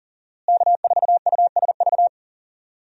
Morse code, heard as a keyed tone, about 40 wpm, 700 Hz. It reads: K4USV
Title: USA Call Signs - 5 Characters - Repeated 2x - 40wpm